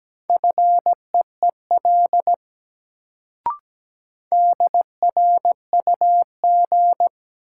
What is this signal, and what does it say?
Morse code, heard as a keyed tone, about 17 wpm, 700 Hz. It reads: FEEL DRUG
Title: Sets of 5 Words - New Word Drill - 17wpm